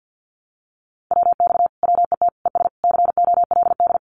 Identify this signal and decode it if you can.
Morse code, heard as a keyed tone, about 50 wpm, 700 Hz. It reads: WX WET ES COLD
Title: Common QSO Phrases Encore 50wpm